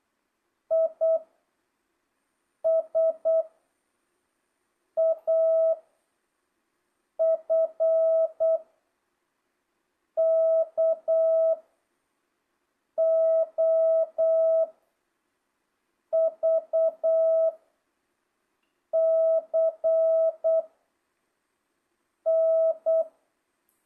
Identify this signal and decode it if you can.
Morse code, heard as a keyed tone, about 8 wpm, 660 Hz. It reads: I S A F K O V C N
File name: jh8cbh / 2025cw